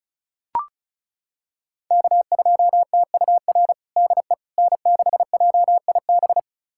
Morse code, heard as a keyed tone, about 36 wpm, 700 Hz. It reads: K2TUR DE N6JIB
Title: USA Callsign DE Callsign - 5 Characters - 35wpm